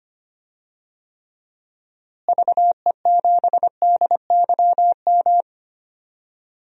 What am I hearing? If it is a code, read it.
Morse code, heard as a keyed tone, about 25 wpm, 700 Hz. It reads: VE7DYM
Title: DXCC-100 Call Signs - 6 Characters - Repeated 2x - 25wpm